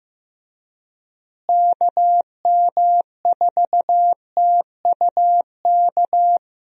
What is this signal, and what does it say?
Morse code, heard as a keyed tone, about 15 wpm, 700 Hz. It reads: KM4TUK
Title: USA Call Signs - 6 Characters - Repeated 3x - 15wpm